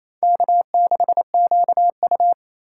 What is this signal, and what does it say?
Morse code, heard as a keyed tone, about 28 wpm, 700 Hz. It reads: K6QU
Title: USA Call Signs - 2x Character Spacing - 28wpm